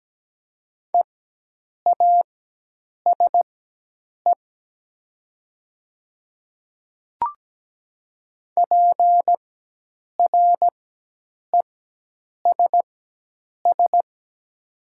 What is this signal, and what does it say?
Morse code, heard as a keyed tone, about 17 wpm, spaced as for 8 wpm, 700 Hz. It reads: EASE PRESS
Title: 4x Character Spacing - Top 500 Words - 17wpm